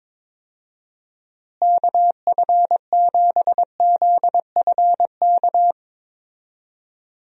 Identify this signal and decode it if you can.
Morse code, heard as a keyed tone, about 22 wpm, 700 Hz. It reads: KF7ZFK
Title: USA Call Signs - 6 Characters - Repeated 3x - 22wpm